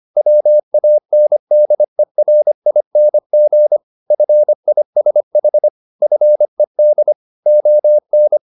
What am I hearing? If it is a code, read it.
Morse code, heard as a keyed tone, about 25 wpm, 600 Hz. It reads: WANDERING FISH FED ON
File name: the old man and the sea part 1 25wpm